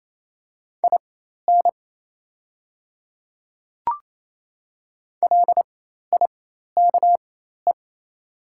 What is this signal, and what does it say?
Morse code, heard as a keyed tone, about 28 wpm, spaced as for 13 wpm, 700 Hz. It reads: IN LIKE
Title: 4x Character Spacing - Top 100 Words - 28wpm